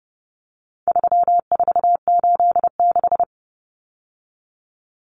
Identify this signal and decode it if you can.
Morse code, heard as a keyed tone, about 30 wpm, 700 Hz. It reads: 3486
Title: Common Cut Numbers - 4-Digit 30wpm